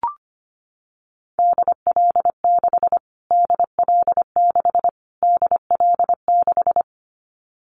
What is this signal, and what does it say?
Morse code, heard as a keyed tone, about 25 wpm, 700 Hz. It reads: DL6 DL6 DL6
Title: Callsign Prefixes for Top 25 DXCC - Repeated 3x - 25wpm